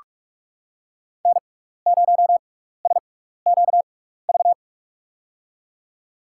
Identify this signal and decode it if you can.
Morse code, heard as a keyed tone, about 48 wpm, spaced as for 16 wpm, 700 Hz. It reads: N0SQV
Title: USA Call Signs - 5 Characters - 6x Character Spacing - 45wpm